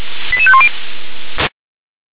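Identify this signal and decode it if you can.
Five-tone selective call (ZVEI), heard as quick runs of five tones, tones of about 70 ms. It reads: 88510
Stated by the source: Five-tone ZVEI selective call received by SDR.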